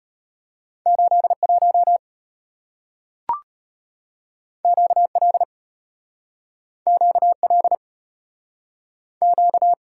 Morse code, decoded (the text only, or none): 81 QL QL Q